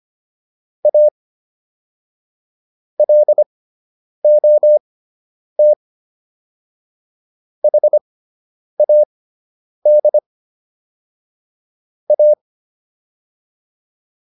Morse code, heard as a keyed tone, about 25 wpm, spaced as for 9 wpm, 600 Hz. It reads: A LOT HAD A